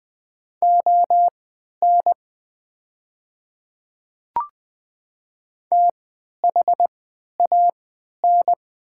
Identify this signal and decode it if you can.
Morse code, heard as a keyed tone, about 20 wpm, spaced as for 11 wpm, 700 Hz. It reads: ON THAN
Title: 3x Character Spacing - Top 100 Words - 20wpm